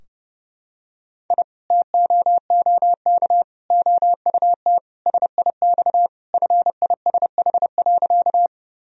Morse code, heard as a keyed tone, about 30 wpm, 700 Hz. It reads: I TOOK OUT SIX FISH.
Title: Sentences from Top 1000 Words - Review Part 2 - 30wpm